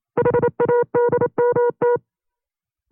Morse code, heard as a keyed tone, about 27 wpm, 480 Hz. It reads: HADMT
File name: CW-practice/25WPM